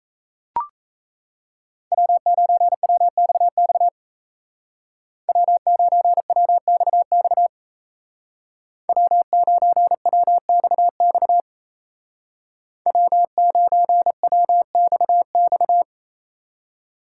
Morse code, heard as a keyed tone, about 34 wpm, 700 Hz. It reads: W9WXX W9WXX W9WXX W9WXX
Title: Speed Racing - USA 1x3 Call Signs 28wpm